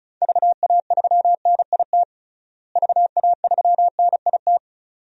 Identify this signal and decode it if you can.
Morse code, heard as a keyed tone, about 36 wpm, 700 Hz. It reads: VA3NIT VA3NIT
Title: DXCC-100 Call Signs - 6 Characters - Repeated 2x - 35wpm